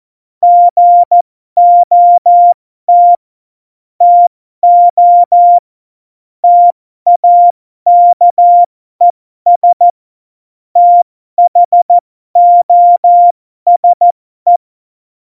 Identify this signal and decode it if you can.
Morse code, heard as a keyed tone, about 14 wpm, spaced as for 12 wpm, 700 Hz. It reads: GOT TO TAKES THOSE
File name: Morsekurs / Witze